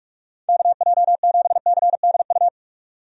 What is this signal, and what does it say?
Morse code, heard as a keyed tone, about 46 wpm, 700 Hz. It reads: KJ7CDU
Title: USA Call Signs - 6 Characters - 45wpm